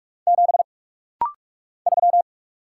Morse code, heard as a keyed tone, about 44 wpm, 700 Hz. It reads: 7 3
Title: Single Letter-Number Alt - Rapid-Fire 45wpm